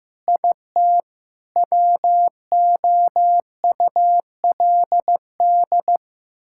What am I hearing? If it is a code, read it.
Morse code, heard as a keyed tone, about 15 wpm, 700 Hz. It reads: IT WOULD
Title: Sentences from Top 1000 Words - Review Part 6 - 15wpm